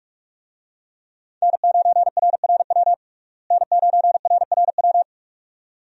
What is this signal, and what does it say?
Morse code, heard as a keyed tone, about 46 wpm, 700 Hz. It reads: N9RRW N9RRW
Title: USA Call Signs - 5 Characters - Repeated 2x - 45wpm